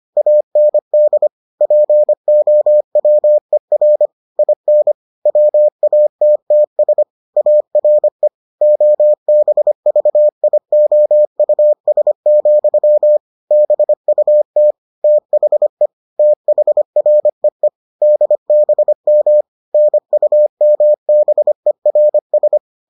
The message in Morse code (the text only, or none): AND POWER IN WATTS ARE OBVIOUS, BUT THE THREE DBM NUMBERS